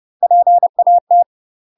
Morse code, heard as a keyed tone, about 30 wpm, 700 Hz. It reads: PAT
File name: UK MASTER QSOs TRAINING / 30WPM / UK MASTER QSOs 0008